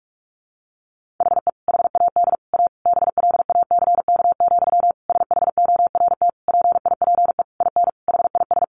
Morse code, heard as a keyed tone, about 44 wpm, 700 Hz. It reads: HE HAD A BLACK, SHORT PIPE IN HIS
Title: Sentences from Top 1500 Words 45wpm